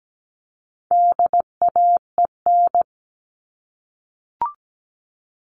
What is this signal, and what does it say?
Morse code, heard as a keyed tone, about 17 wpm, 700 Hz. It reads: DAEN
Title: All Cut Numbers - 4-Digit 17wpm